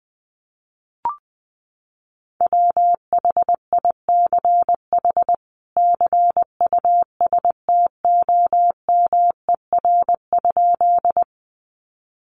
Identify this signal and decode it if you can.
Morse code, heard as a keyed tone, about 20 wpm, 700 Hz. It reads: WHICH CUSTOMER?
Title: Sentences from Top 1100 Words 20wpm